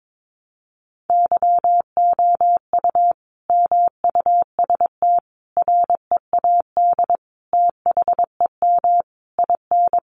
Morse code, heard as a keyed tone, about 22 wpm, 700 Hz. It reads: YOU MUST READ THEM IN
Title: Sentences from Top 900 Words 22wpm